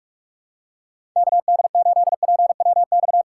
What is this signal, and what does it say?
Morse code, heard as a keyed tone, about 46 wpm, 700 Hz. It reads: KD8PWX
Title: USA Call Signs - 6 Characters - 2x Character Spacing - 45wpm